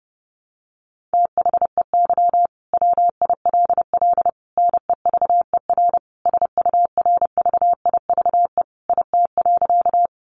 Morse code, heard as a keyed tone, about 30 wpm, 700 Hz. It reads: THEY WILL NEVER SURVIVE IT.
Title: Sentences from Top 1300 Words 30wpm